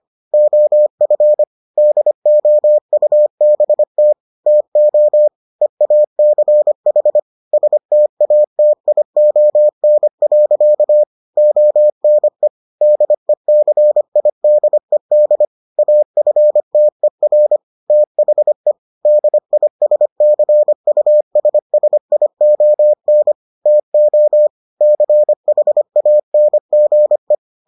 Morse code, heard as a keyed tone, about 25 wpm, 600 Hz. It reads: OF DOUBT TO EACH STATION. ONE DECIDED AFTER THE DISCUSSION TO CHANGE